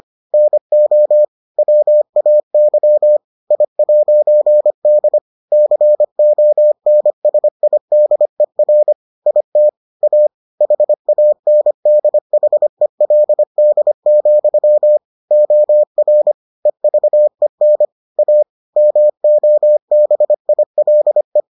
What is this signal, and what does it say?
Morse code, heard as a keyed tone, about 25 wpm, 600 Hz. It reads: NO WAY I'D CONSIDER IT A HANDHELD, OR EVEN A MOBILE